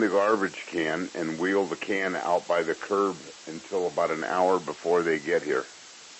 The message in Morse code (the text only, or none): none